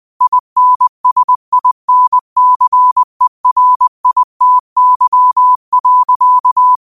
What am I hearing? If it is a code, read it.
Morse code, heard as a keyed tone, about 20 wpm, 1000 Hz. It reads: INSINCERITY.